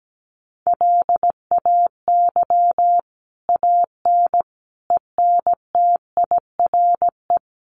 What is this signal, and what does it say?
Morse code, heard as a keyed tone, about 17 wpm, 700 Hz. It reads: LAY AN ENTIRE
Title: A Sound Of Thunder 17wpm